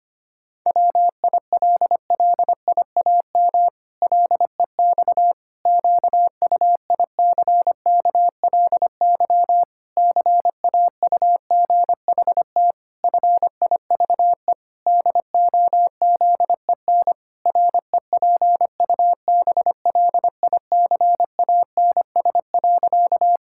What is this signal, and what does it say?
Morse code, heard as a keyed tone, about 25 wpm, 700 Hz. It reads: WILLIAM LEX QUICKLY CAUGHT FIVE DOZEN REPUBLICANS.
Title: Pangram Challenge v2 - 25wpm